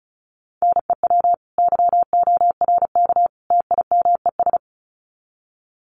Morse code, heard as a keyed tone, about 34 wpm, 700 Hz. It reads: NEW YORK TIMES
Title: Sets of 3 Words 35wpm